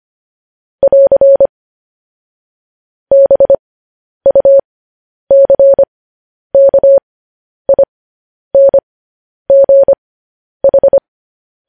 Morse code, heard as a keyed tone, about 25 wpm, spaced as for 10 wpm, 550 Hz. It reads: <AR> BUCKINGH
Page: news headlines in morse code at 10 wpm